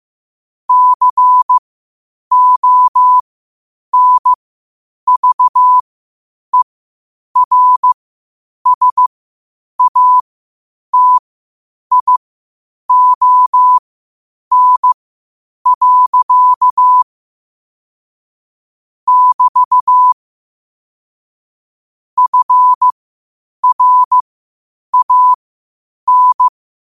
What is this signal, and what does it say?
Morse code, heard as a keyed tone, about 15 wpm, spaced as for 8 wpm, 1000 Hz. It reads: CONVERSATION. <BT> FRAN
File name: qotdfiles / 8wpm